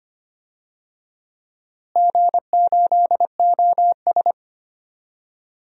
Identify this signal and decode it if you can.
Morse code, heard as a keyed tone, about 25 wpm, 700 Hz. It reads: G8OS
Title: DXCC-100 Call Signs - 4 Characters - Repeated 2x - 25wpm